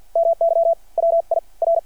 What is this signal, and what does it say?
Morse code, heard as a keyed tone, about 52 wpm, spaced as for 37 wpm, 660 Hz. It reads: MY WIR
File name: archived / CRSnet